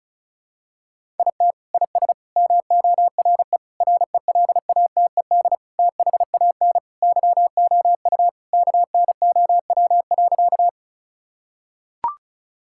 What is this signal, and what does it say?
Morse code, heard as a keyed tone, about 36 wpm, 700 Hz. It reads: IT IS MORE RELATED THAN YOU KNOW.
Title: Sentences from Top 1000 Words - Review Part 5 - 35wpm